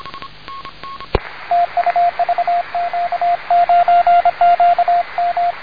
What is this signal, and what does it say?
Morse code, heard as a keyed tone, about 25 wpm, 700 Hz, with radio noise. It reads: TU VQ9QM
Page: Contact with VQ9QM on 28 MHz band CW, power into antenna was 1 Watt